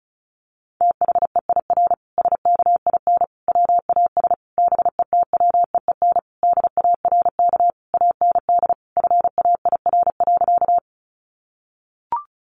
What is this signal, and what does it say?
Morse code, heard as a keyed tone, about 34 wpm, 700 Hz. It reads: THEIR SKIN WAS BETWEEN DARK AND FAIR.
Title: Sentences from Top 1300 Words 35wpm